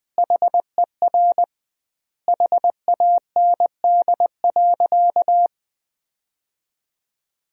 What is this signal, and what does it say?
Morse code, heard as a keyed tone, about 20 wpm, 700 Hz. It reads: HER HAND.
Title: 2x Word-Spacing - Sentences from Top 500 Words - 20wpm